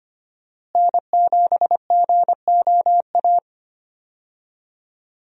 Morse code, heard as a keyed tone, about 25 wpm, 700 Hz. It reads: N7GOA
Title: USA Call Signs - 5 Characters - Repeated 2x - 25wpm